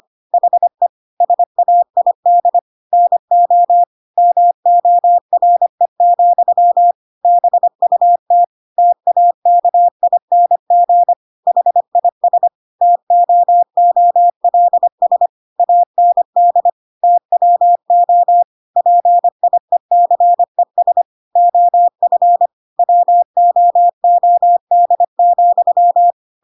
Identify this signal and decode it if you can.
Morse code, heard as a keyed tone, about 25 wpm, 700 Hz. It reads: HE SAID NO MORE, BUT TAKING HIS TOOLS AND TWO PIECES OF WOOD,
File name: Pinocchio / Pinocchio ch8 25 25 ik1hge